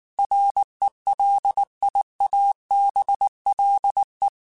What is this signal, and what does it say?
Morse code, heard as a keyed tone, about 19 wpm, 800 Hz. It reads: RELIABLE
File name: QSOs / QSOs-mp3s / WD0FBT 1@18